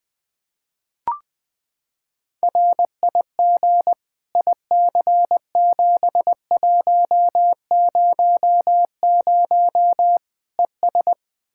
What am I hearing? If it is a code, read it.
Morse code, heard as a keyed tone, about 20 wpm, 700 Hz. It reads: RIG IC7100 ES
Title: Practice QSO - Exchange 2 - Info - 20wpm